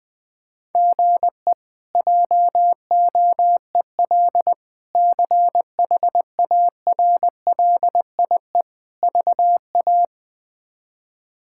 Morse code, heard as a keyed tone, about 21 wpm, 700 Hz. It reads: GE JOEL CHARLIE VA
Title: K1USN SST QSO Practice - 20wpm